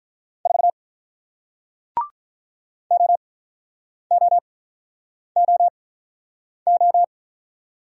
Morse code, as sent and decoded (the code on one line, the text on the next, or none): ....- / --- --- --- ---
4 OOOO